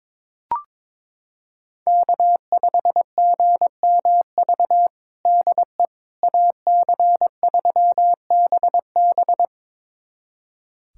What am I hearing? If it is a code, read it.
Morse code, heard as a keyed tone, about 22 wpm, 700 Hz. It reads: K5GMV DE AC3BB
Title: USA Callsign DE Callsign - 5 Characters - 22wpm